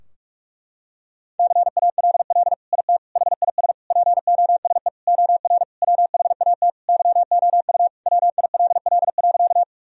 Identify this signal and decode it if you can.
Morse code, heard as a keyed tone, about 46 wpm, 700 Hz. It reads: CALL IT HIS POSE OR WHAT YOU WILL.
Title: Sentences from Top 1000 Words - Review Part 2 - 45wpm